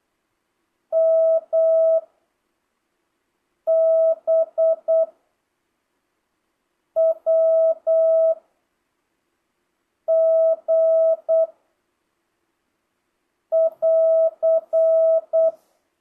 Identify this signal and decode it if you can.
Morse code, heard as a keyed tone, about 8 wpm, 650 Hz. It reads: M B W G <AR>
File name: jh8cbh / 2025cw